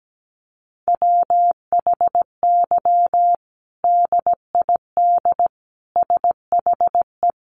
Morse code, heard as a keyed tone, about 17 wpm, 700 Hz. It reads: WHY DID SHE
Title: Sentences from Top 1300 Words 17wpm